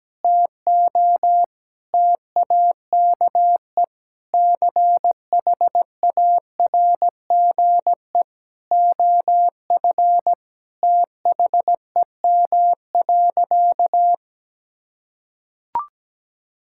Morse code, heard as a keyed tone, about 17 wpm, 700 Hz. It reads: TO TAKE CHARGE OF THEM.
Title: Sentences from Top 1000 Words - Review Part 6 - 17wpm